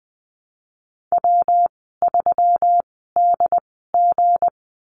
Morse code, heard as a keyed tone, about 20 wpm, spaced as for 15 wpm, 700 Hz. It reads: W3DG
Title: 173 Lesson 28g - Number 9 - Call Signs 20wpm